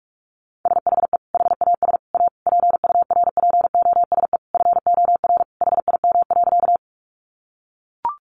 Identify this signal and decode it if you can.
Morse code, heard as a keyed tone, about 44 wpm, 700 Hz. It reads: SHE HAS A PURPOSE FOR HIM.
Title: Sentences from Top 800 Words 45wpm